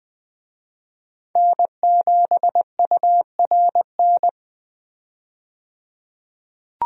Morse code, heard as a keyed tone, about 20 wpm, 700 Hz. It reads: N7URN
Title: USA Call Signs - 5 Characters - 20wpm